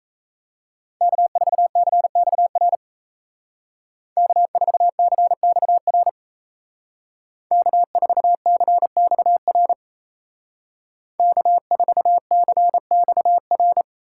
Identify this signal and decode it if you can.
Morse code, heard as a keyed tone, about 33 wpm, 700 Hz. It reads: K4CXR K4CXR K4CXR K4CXR
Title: Speed Racing - USA 1x3 Call Signs 28wpm